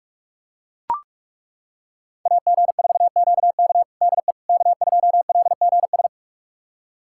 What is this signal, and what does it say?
Morse code, heard as a keyed tone, about 41 wpm, 700 Hz. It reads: AG4QK DE K2LGS
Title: USA Callsign DE Callsign - 5 Characters - 45wpm